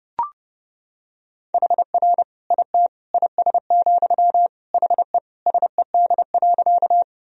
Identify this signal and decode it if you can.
Morse code, heard as a keyed tone, about 30 wpm, 700 Hz. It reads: HR IT IS, HE SED.
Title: Sentences from Top 200 Words and Abbreviations 30wpm